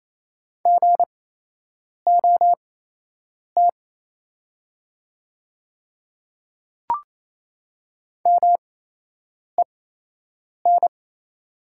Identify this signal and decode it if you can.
Morse code, heard as a keyed tone, about 28 wpm, spaced as for 8 wpm, 700 Hz. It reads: GOT MEN
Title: 8x Character Spacing - Top 300 Words - 28wpm